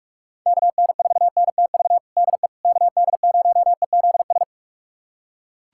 Morse code, heard as a keyed tone, about 41 wpm, 700 Hz. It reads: KN4NTV DE KD0EZS
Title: USA Callsign DE Callsign 45wpm